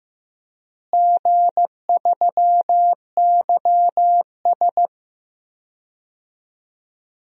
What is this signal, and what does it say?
Morse code, heard as a keyed tone, about 15 wpm, 700 Hz. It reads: G3YS
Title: DXCC-100 Call Signs - 4 Characters - Repeated 2x - 15wpm